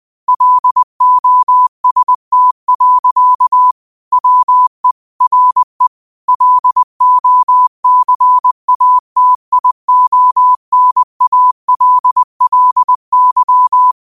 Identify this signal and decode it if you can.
Morse code, heard as a keyed tone, about 20 wpm, 1000 Hz. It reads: LOST. WE RE LOCATIONALLY
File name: qotdfiles / 20wpm